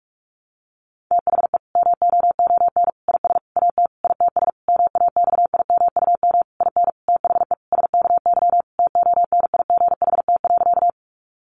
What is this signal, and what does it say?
Morse code, heard as a keyed tone, about 44 wpm, 700 Hz. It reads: THE MOON IS AT ITS MAXIMUM IN THE SKY TONIGHT.
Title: Sentences from Top 1300 Words 45wpm